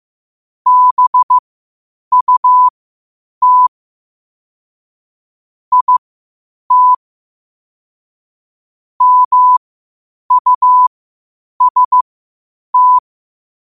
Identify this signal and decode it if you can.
Morse code, heard as a keyed tone, about 15 wpm, spaced as for 8 wpm, 1000 Hz. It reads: BUT IT MUST
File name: qotdfiles / 8wpm